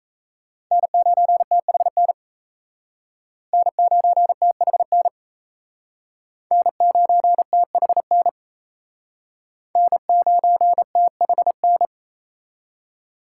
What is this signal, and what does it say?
Morse code, heard as a keyed tone, about 35 wpm, 700 Hz. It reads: N9THN N9THN N9THN N9THN